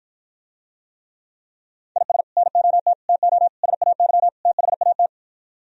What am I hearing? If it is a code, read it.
Morse code, heard as a keyed tone, about 57 wpm, 700 Hz. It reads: IS NOT TO SAY THAT